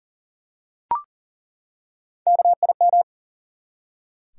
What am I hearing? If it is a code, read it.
Morse code, heard as a keyed tone, about 40 wpm, 700 Hz. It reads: KIM
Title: USA Common Names - Max 3 Letters - 40wpm